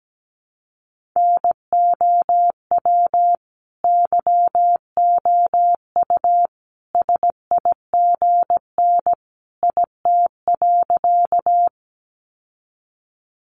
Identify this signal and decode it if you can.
Morse code, heard as a keyed tone, about 17 wpm, 700 Hz. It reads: NOW YOU SIGN IT.
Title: Sentences from Top 1000 Words 17wpm